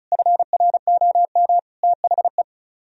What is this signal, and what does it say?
Morse code, heard as a keyed tone, about 46 wpm, 700 Hz. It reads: FROM THE